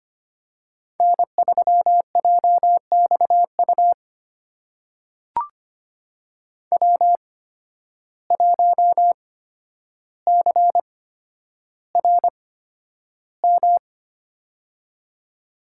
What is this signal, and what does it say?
Morse code, heard as a keyed tone, about 25 wpm, spaced as for 11 wpm, 700 Hz. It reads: N3JXU W1CRM
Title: USA Call Signs - 5 Characters - 8x Character Spacing - 25wpm